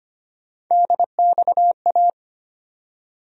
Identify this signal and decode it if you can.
Morse code, heard as a keyed tone, about 25 wpm, 700 Hz. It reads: DXA
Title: USA Call Sign Postfixes - 3 Letters 25wpm